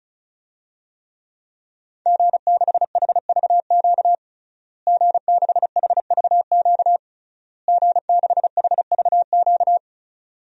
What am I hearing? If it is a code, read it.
Morse code, heard as a keyed tone, about 36 wpm, 700 Hz. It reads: G6HVQ G6HVQ G6HVQ
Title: DXCC-100 Call Signs - 5 Characters - Repeated 3x - 35wpm